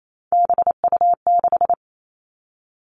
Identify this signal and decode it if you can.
Morse code, heard as a keyed tone, about 28 wpm, 700 Hz. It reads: BU6
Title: Three Challenging Letter-Number Combinations 28wpm